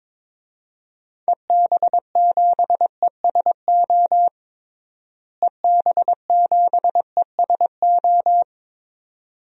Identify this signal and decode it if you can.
Morse code, heard as a keyed tone, about 21 wpm, 700 Hz. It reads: EB7ESO EB7ESO
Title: DXCC-100 Call Signs - 6 Characters - Repeated 2x - 22wpm